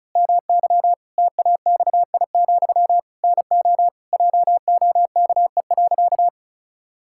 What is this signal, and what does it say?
Morse code, heard as a keyed tone, about 34 wpm, 700 Hz. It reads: MY TAXI, NO JOKE.
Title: Pangram Challenge v2 - 35wpm